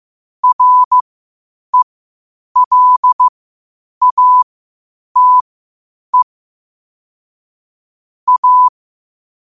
Morse code, heard as a keyed tone, about 15 wpm, spaced as for 8 wpm, 1000 Hz. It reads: RELATE A
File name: 8wpm